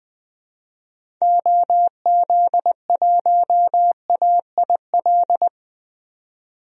Morse code, heard as a keyed tone, about 20 wpm, 700 Hz. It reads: OZ1AIL
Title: DXCC-100 Call Signs - 6 Characters - 20wpm